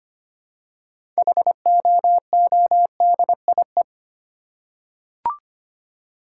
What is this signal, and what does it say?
Morse code, heard as a keyed tone, about 25 wpm, 700 Hz. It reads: HOODIE
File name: practice / 25wpm / Difficult Words with Hints - 25wpm